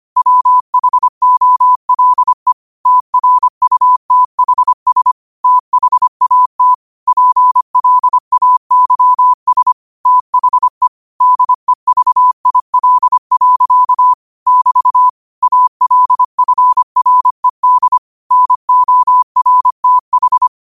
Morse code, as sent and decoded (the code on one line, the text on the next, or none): .-- .... --- .-.. / - .-. ..- - .... ... / - .... .- - / .--. .-.. .- -.-- ... / - .... / -.. ...- .. .-.. .-.-.- / -...- / .- .-.. ..-. .-. -.. / -. --- .-. - ....
WHOLE TRUTHS THAT PLAYS THE DEVIL. <BT> ALFRED NORTH